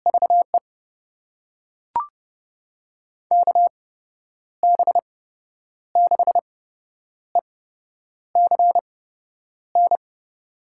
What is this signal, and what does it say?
Morse code, heard as a keyed tone, about 30 wpm, spaced as for 9 wpm, 700 Hz. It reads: VE KB6ECN
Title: USA Call Signs - 6 Characters - 8x Character Spacing - 30wpm